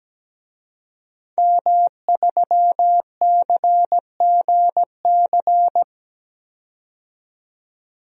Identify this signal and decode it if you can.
Morse code, heard as a keyed tone, about 17 wpm, 700 Hz. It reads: M3CGC
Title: DXCC-100 Call Signs - 5 Characters - 17wpm